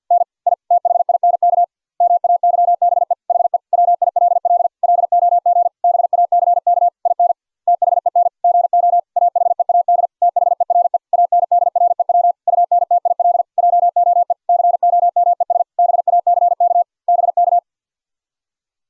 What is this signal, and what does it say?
Morse code, heard as a keyed tone, about 50 wpm, 700 Hz. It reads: N I THINK MAYBE HE WILL LOG BACK IN THEN GO AHEAD THERE ANDREW UNTIL JOE COMES BACK BK